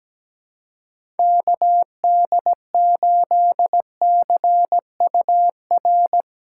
Morse code, heard as a keyed tone, about 17 wpm, 700 Hz. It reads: KD8CUR
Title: USA Call Signs - 3x Character Spacing - 17wpm